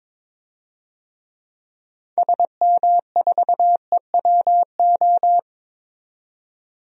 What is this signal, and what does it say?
Morse code, heard as a keyed tone, about 22 wpm, 700 Hz. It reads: SM4EWO
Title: DXCC-100 Call Signs - 6 Characters - 22wpm